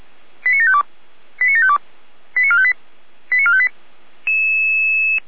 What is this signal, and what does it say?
Five-tone selective call (ZVEI), heard as quick runs of five tones, tones of about 70 ms. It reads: 89752 89752 89458 89458
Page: SDR/ZVEI